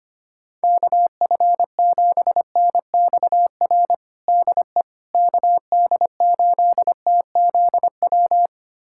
Morse code, heard as a keyed tone, about 25 wpm, 700 Hz. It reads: KF7NXR DE KD8TZW
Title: USA Callsign DE Callsign - 6 Characters - 25wpm